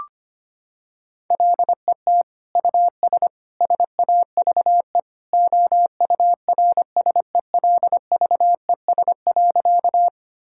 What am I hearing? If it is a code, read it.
Morse code, heard as a keyed tone, about 25 wpm, 700 Hz. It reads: LET US SAVE OURSELVES.
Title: Sentences from Top 1600 Words 25wpm